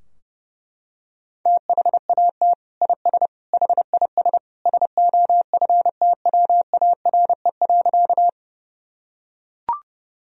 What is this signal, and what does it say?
Morse code, heard as a keyed tone, about 32 wpm, 700 Hz. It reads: THAT IS HIS SOFTWARE.
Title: Sentences from Top 1000 Words - Review Part 3 - 30wpm